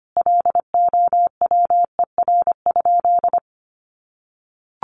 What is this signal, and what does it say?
Morse code, heard as a keyed tone, about 25 wpm, 700 Hz. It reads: LOWER?
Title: Sentences from Top 700 Words 25wpm